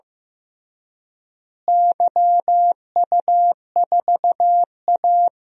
Morse code, heard as a keyed tone, about 15 wpm, 700 Hz. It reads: YU4A
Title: DXCC-100 Call Signs - 4 Characters - Repeated 3x - 15wpm